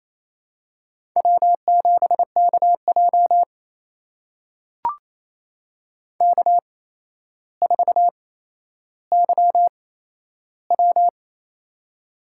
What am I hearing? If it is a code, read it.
Morse code, heard as a keyed tone, about 27 wpm, spaced as for 12 wpm, 700 Hz. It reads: W7KJ K4YW
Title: USA Call Signs - 4 Characters - 8x Character Spacing - 28wpm